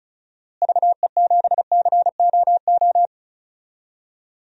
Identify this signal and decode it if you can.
Morse code, heard as a keyed tone, about 35 wpm, 700 Hz. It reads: VE7COO
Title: DXCC-100 Call Signs - 6 Characters - Repeated 3x - 35wpm